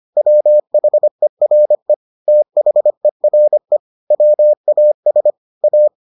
Morse code, heard as a keyed tone, about 25 wpm, 600 Hz. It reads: WHERE THERE WAS A